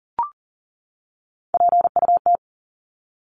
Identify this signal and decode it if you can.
Morse code, heard as a keyed tone, about 40 wpm, 700 Hz. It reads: PUT
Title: ING Suffix - Top 100 Words with Base Words 40wpm